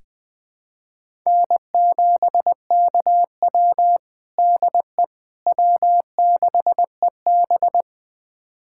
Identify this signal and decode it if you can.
Morse code, heard as a keyed tone, about 21 wpm, 700 Hz. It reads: N7KW DE W6EB
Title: USA Callsign DE Callsign - 4 Characters - 20wpm